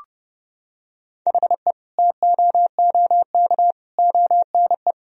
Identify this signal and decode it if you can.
Morse code, heard as a keyed tone, about 32 wpm, 700 Hz. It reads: HE TOOK ONE
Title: Sentences from Top 1000 Words - Review Part 1 - 30wpm